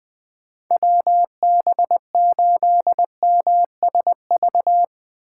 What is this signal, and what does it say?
Morse code, heard as a keyed tone, about 22 wpm, 700 Hz. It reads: WB8MSV
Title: Speed Racing - USA 2x3 Call Signs 20wpm